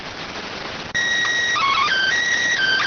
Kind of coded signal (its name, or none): SSTV (slow-scan television)